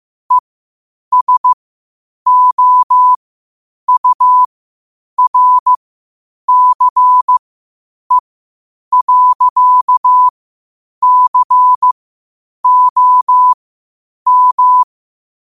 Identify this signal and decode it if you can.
Morse code, heard as a keyed tone, about 15 wpm, spaced as for 8 wpm, 1000 Hz. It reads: ESOURCE.COM
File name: qotdfiles / 8wpm